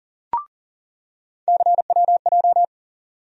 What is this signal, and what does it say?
Morse code, heard as a keyed tone, about 40 wpm, 700 Hz. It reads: CWJ
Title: USA Call Sign Postfixes - 3 Letters 40wpm